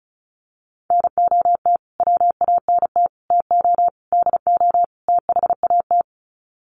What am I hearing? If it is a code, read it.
Morse code, heard as a keyed tone, about 34 wpm, 700 Hz. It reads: NOT WANT TO DO THAT
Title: Sets of 5 Words Encore 35wpm